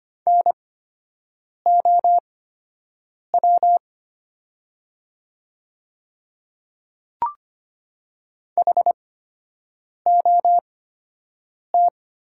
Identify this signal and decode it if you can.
Morse code, heard as a keyed tone, about 25 wpm, spaced as for 7 wpm, 700 Hz. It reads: NOW HOT
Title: 8x Character Spacing - Top 500 Words - 25wpm